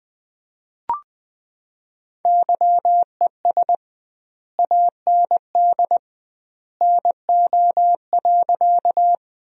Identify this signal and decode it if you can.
Morse code, heard as a keyed tone, about 21 wpm, 700 Hz. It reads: YES AND NO.
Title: 2x Word-Spacing - Sentences from Top 500 Words - 20wpm